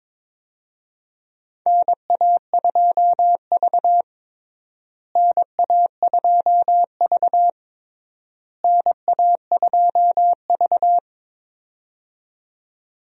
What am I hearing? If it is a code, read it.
Morse code, heard as a keyed tone, about 22 wpm, 700 Hz. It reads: NA2V NA2V NA2V
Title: USA Call Signs - 4 Characters - Repeated 3x - 22wpm